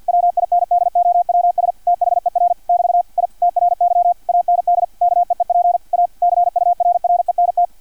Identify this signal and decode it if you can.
Morse code, heard as a keyed tone, about 50 wpm, 700 Hz, with radio noise. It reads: WINDOWS THER <BT> I TRY AND KEEP A CURRENT
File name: archived / CRSnet